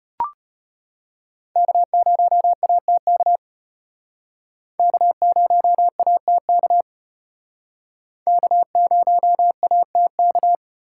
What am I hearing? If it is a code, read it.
Morse code, heard as a keyed tone, about 32 wpm, 700 Hz. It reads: K0ATK K0ATK K0ATK